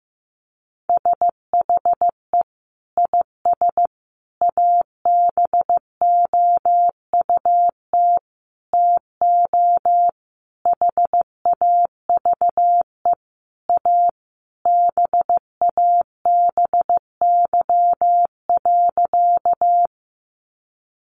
Morse code, heard as a keyed tone, about 15 wpm, 700 Hz. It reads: SHE IS ABOUT TO HAVE A BABY.
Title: Sentences from Top 1300 Words 15wpm